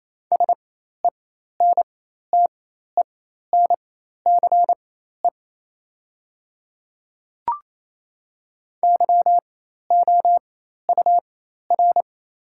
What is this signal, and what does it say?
Morse code, heard as a keyed tone, about 28 wpm, spaced as for 13 wpm, 690 Hz. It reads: SENTENCE YOUR
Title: 4x Character Spacing - Top 200 Words - 28wpm